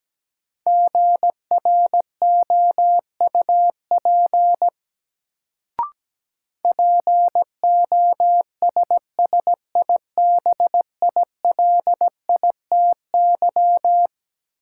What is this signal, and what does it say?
Morse code, heard as a keyed tone, about 17 wpm, 700 Hz. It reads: GROUP POSSIBILITY
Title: Sets of 3 Words - New Word Drill - 17wpm